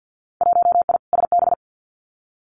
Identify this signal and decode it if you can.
Morse code, heard as a keyed tone, about 50 wpm, 700 Hz. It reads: 1I SB
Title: Winter Field Day Exchanges 50wpm